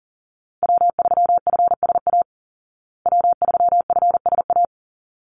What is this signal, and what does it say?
Morse code, heard as a keyed tone, about 37 wpm, 700 Hz. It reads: W3FSA W3FSA
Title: Call Signs Repeated 3x - 40wpm